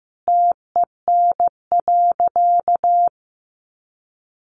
Morse code, heard as a keyed tone, about 15 wpm, 700 Hz. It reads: TEN.
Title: Sentences from Top 1300 Words 15wpm